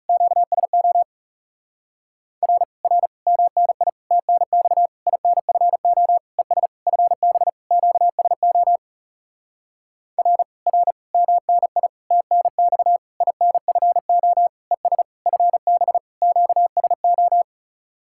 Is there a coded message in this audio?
Yes: Morse code from 0.1 to 17.4 s, heard as a keyed tone, about 40 wpm, 700 Hz. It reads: QSO R R MNI TNX INFO ES FB QSO R R MNI TNX INFO ES FB QSO